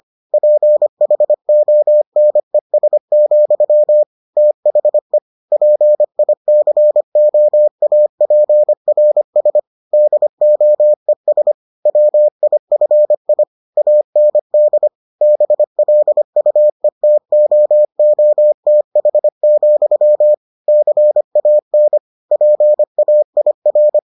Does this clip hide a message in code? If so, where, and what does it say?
Yes: Morse code from 0.3 to 24.0 s, heard as a keyed tone, about 25 wpm, 600 Hz. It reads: PHONES, THE PICOAPRS DOES WIFI AND BLUETOOTH, CAN PAIR